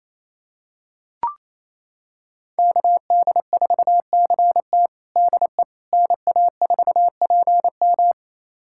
Morse code, heard as a keyed tone, about 28 wpm, 700 Hz. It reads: KD4CT DE NA4PM
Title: USA Callsign DE Callsign - 5 Characters - 28wpm